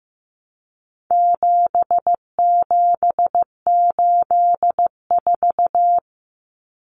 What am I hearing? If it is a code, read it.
Morse code, heard as a keyed tone, about 15 wpm, 700 Hz. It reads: 7784
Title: Common Cut Numbers - 4-Digit 15wpm